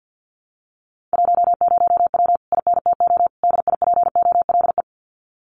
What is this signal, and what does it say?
Morse code, heard as a keyed tone, about 50 wpm, 700 Hz. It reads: 10W INTO DIPOLE
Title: Common QSO Phrases Encore 50wpm